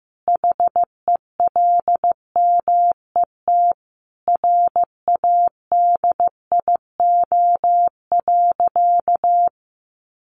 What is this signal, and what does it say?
Morse code, heard as a keyed tone, about 15 wpm, 700 Hz. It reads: HELMET RADIO.
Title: A Sound Of Thunder 15wpm